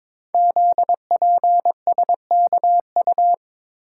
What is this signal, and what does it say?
Morse code, heard as a keyed tone, about 22 wpm, 700 Hz. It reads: ZPSKU
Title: Code Groups - 5 Letter:Numbers - 22wpm